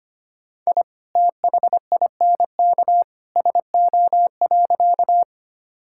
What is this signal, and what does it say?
Morse code, heard as a keyed tone, about 25 wpm, 700 Hz. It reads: I THINK SO.
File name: practice / 25wpm / Sentences with 2-Word Phrases - 25wpm